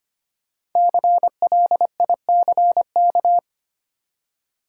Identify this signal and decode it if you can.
Morse code, heard as a keyed tone, about 25 wpm, 700 Hz. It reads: CLICK
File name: 5 Letter Words - First 3 Letters Given - 25wpm